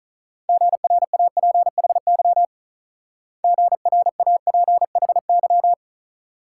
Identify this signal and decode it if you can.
Morse code, heard as a keyed tone, about 43 wpm, 700 Hz. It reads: GRAPHY GRAPHY
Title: Speed-Racing Multi-letter Phonemes - 35wpm